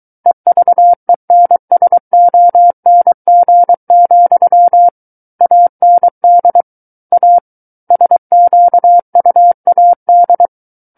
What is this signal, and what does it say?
Morse code, heard as a keyed tone, about 23 wpm, 700 Hz. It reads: EVENSONG, AND A SQUAD
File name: Book One / 14-Chapter 14 - 23 WPM